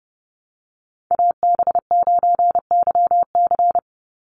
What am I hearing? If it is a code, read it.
Morse code, heard as a keyed tone, about 30 wpm, 700 Hz. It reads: AB9YC
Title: Call Signs - 30wpm